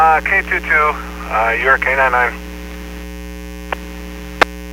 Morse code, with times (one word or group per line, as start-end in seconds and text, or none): none